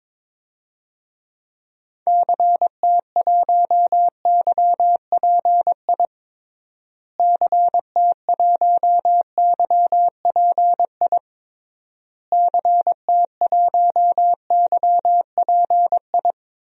Morse code, heard as a keyed tone, about 22 wpm, 700 Hz. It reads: CT1YPI CT1YPI CT1YPI
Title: DXCC-100 Call Signs - 6 Characters - Repeated 3x - 22wpm